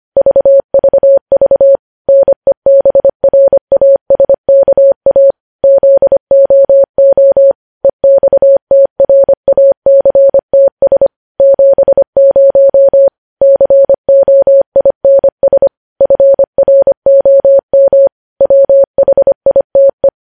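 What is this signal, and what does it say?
Morse code, heard as a keyed tone, about 25 wpm, 550 Hz. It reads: VVV NEBRASKA ZOO EXTRACTS 70 COINS FROM WHITE
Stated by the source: Morse news headlines sent at 25 words per minute.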